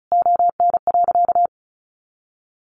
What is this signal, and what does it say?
Morse code, heard as a keyed tone, about 34 wpm, 700 Hz. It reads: ON.
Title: Easy Phrases 35wpm